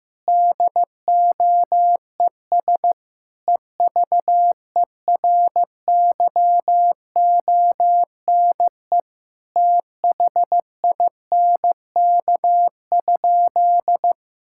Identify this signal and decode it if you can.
Morse code, heard as a keyed tone, about 15 wpm, 700 Hz. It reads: DOES EVERYONE THINK?
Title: Sentences from Top 1000 Words - Review Part 5 - 15wpm